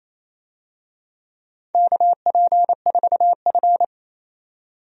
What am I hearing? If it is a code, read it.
Morse code, heard as a keyed tone, about 28 wpm, 700 Hz. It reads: KP4F
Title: DXCC-100 Call Signs - 4 Characters - 28wpm